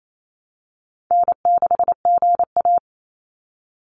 Morse code, heard as a keyed tone, about 28 wpm, 700 Hz. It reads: N6GA
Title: Easy Call Signs - 28wpm